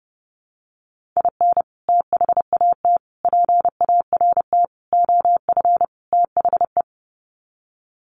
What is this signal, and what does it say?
Morse code, heard as a keyed tone, about 30 wpm, 700 Hz. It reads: IN THAT PART OF THE
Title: Sets of 5 Words Encore2 30wpm